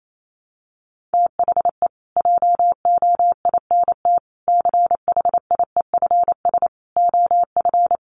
Morse code, heard as a keyed tone, about 28 wpm, 700 Hz. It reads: THE JOINT CHIEFS OF
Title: Sets of 5 Words Encore2 28wpm